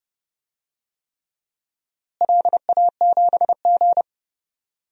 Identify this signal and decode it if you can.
Morse code, heard as a keyed tone, about 30 wpm, 700 Hz. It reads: LA7G
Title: DXCC-100 Call Signs - 4 Characters - Repeated 3x - 30wpm